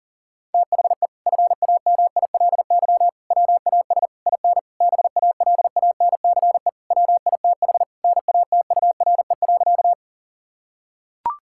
Morse code, heard as a keyed tone, about 37 wpm, 700 Hz. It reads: THE FAMILY WAS IN BALANCE WITH NATURE.
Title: Sentences from Top 1000 Words - Review Part 6 - 40wpm